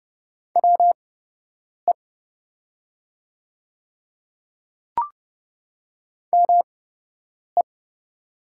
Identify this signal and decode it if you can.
Morse code, heard as a keyed tone, about 30 wpm, spaced as for 8 wpm, 700 Hz. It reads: WE ME
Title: 8x Character Spacing - Top 200 Words - 30wpm